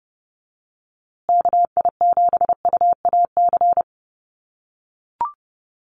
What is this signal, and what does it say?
Morse code, heard as a keyed tone, about 30 wpm, 700 Hz. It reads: KI7UAC
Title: USA Call Signs Repeated 3x 30wpm